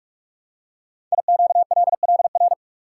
Morse code, heard as a keyed tone, about 44 wpm, 700 Hz. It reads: IQLLR
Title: Code Groups - 5 Letters - 45wpm